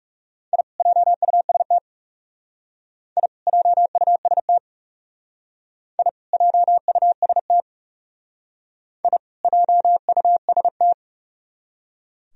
Morse code, heard as a keyed tone, about 37 wpm, 700 Hz. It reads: I JUST I JUST I JUST I JUST
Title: Speed-Racing - Sets of 2 Words 30wpm